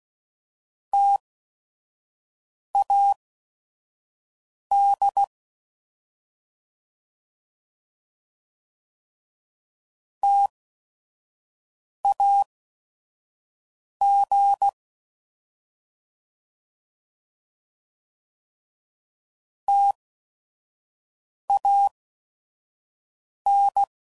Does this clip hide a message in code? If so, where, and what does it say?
Yes: Morse code from 0.9 to 23.8 s, heard as a keyed tone, about 16 wpm, spaced as for 5 wpm, 800 Hz. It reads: TAD TAG TAN